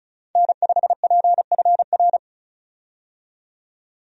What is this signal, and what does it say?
Morse code, heard as a keyed tone, about 34 wpm, 700 Hz. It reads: N5PFR
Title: USA Call Signs - 5 Characters - 35wpm